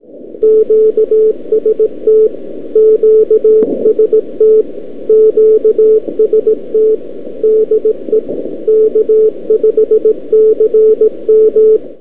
Morse code, heard as a keyed tone, about 18 wpm, 440 Hz, with radio noise. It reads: QST QST QST DE K5CM
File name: k5cm 160m 2-2-12 by abi